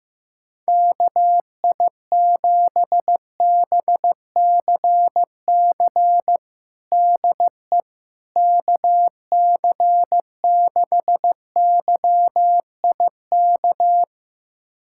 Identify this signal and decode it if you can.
Morse code, heard as a keyed tone, about 15 wpm, 700 Hz. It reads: KI7BCC DE KC6YIK
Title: USA Callsign DE Callsign 15wpm